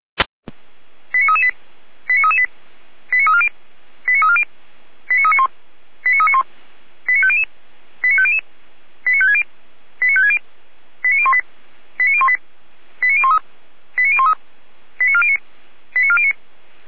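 Five-tone selective call (ZVEI), heard as quick runs of five tones, tones of about 70 ms. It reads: 89338 89338 89340 89340 89391 89391 89500 89500 89560 89560 89018 89018 89013 89013 89409 89409
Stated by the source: Five-tone ZVEI selective call received by SDR, from audio.f85360.feuerwehr-orlamuende.org, a fire brigade (Feuerwehr Orlamünde).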